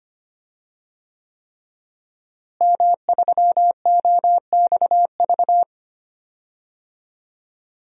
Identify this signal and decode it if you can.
Morse code, heard as a keyed tone, about 25 wpm, 700 Hz. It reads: M3OXV
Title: DXCC-100 Call Signs - 5 Characters - Repeated 3x - 25wpm